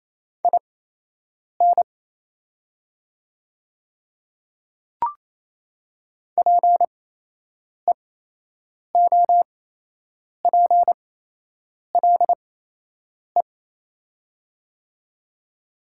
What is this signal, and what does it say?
Morse code, heard as a keyed tone, about 28 wpm, spaced as for 8 wpm, 700 Hz. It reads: IN PEOPLE
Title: 8x Character Spacing - Top 100 Words - 28wpm